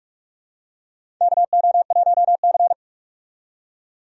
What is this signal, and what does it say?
Morse code, heard as a keyed tone, about 44 wpm, 700 Hz. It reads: KO1C
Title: USA Call Signs - 4 Characters - 45wpm